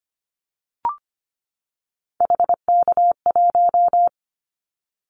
Morse code, heard as a keyed tone, about 25 wpm, 700 Hz. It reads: HK1
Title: Callsign Prefixes for Top 100 DXCC - 25wpm